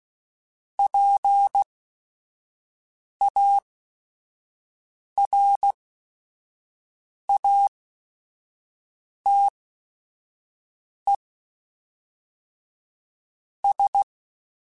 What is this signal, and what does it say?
Morse code, heard as a keyed tone, about 16 wpm, spaced as for 4 wpm, 800 Hz. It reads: PARATES
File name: Technical Subjects / Technical Subjects-mp3s / Sw Pwr Sup @5